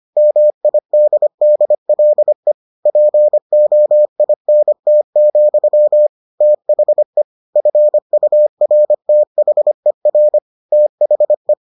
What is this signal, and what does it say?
Morse code, heard as a keyed tone, about 25 wpm, 600 Hz. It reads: MIDDLE POINT, THE FURTHER THE